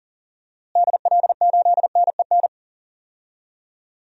Morse code, heard as a keyed tone, about 40 wpm, 700 Hz. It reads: DL8NEN
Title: DXCC-100 Call Signs - 6 Characters - Repeated 3x - 40wpm